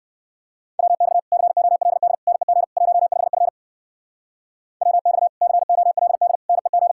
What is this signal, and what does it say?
Morse code, heard as a keyed tone, about 55 wpm, 700 Hz. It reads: WX COLD NEG 15F WX COLD NEG
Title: Speed-Racing Common QSO Phrases - 45wpm